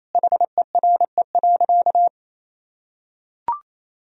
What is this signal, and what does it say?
Morse code, heard as a keyed tone, about 28 wpm, 700 Hz. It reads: HERE.
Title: Sentences from Top 1000 Words - Review Part 1 - 28wpm